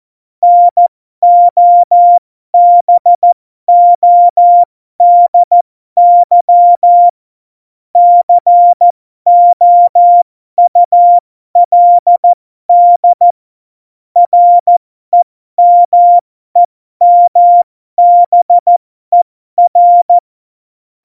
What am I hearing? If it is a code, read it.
Morse code, heard as a keyed tone, about 14 wpm, spaced as for 12 wpm, 700 Hz. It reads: NOBODY COULD REMEMBER
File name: Morsekurs / Witze